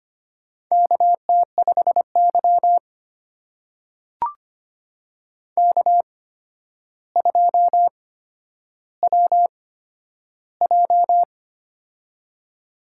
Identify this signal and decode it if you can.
Morse code, heard as a keyed tone, about 25 wpm, spaced as for 11 wpm, 700 Hz. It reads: KT5Y K2WJ
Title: USA Call Signs - 4 Characters - 8x Character Spacing - 25wpm